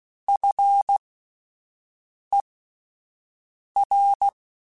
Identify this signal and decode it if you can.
Morse code, heard as a keyed tone, about 16 wpm, spaced as for 5 wpm, 800 Hz. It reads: FER
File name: (spaced) / QSOs Spaced-mp3s / kc7haa @10 sp